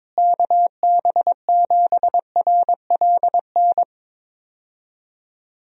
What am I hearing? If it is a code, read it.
Morse code, heard as a keyed tone, about 21 wpm, 700 Hz. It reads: KB7RLN
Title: USA Call Signs - 6 Characters - Repeated 2x - 22wpm